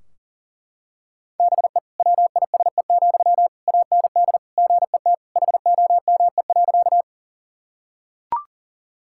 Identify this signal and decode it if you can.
Morse code, heard as a keyed tone, about 40 wpm, 700 Hz. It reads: BE WISE, AND GET HOME.
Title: Sentences from Top 1600 Words 40wpm